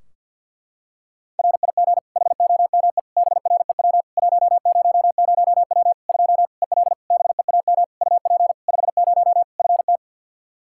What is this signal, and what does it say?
Morse code, heard as a keyed tone, about 49 wpm, 700 Hz. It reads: RIG HOME BREW 100W 2 EL BEAM UP 50 FT
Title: Practice QSO - Exchange 2 - Info - 50wpm